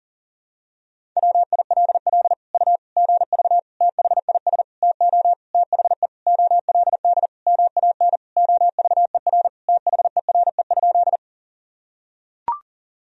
Morse code, heard as a keyed tone, about 40 wpm, 700 Hz. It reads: WILL U GV THIS TO THE OLD MAN OVER THERE?
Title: Sentences from Top 200 Words and Abbreviations 40wpm